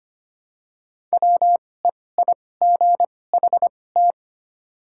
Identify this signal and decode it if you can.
Morse code, heard as a keyed tone, about 25 wpm, spaced as for 18 wpm, 700 Hz. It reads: WEIGHT
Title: 2x Character Spacing - Top 500 Words - 25wpm